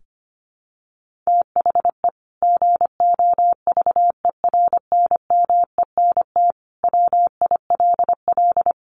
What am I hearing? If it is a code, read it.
Morse code, heard as a keyed tone, about 25 wpm, 700 Hz. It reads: THE GOVERNMENT WILL
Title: Sentences from Top 1500 Words 25wpm